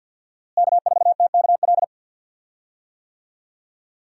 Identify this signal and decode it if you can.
Morse code, heard as a keyed tone, about 48 wpm, 700 Hz. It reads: K4TKL
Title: USA Call Signs - 5 Characters - Repeated 2x - 50wpm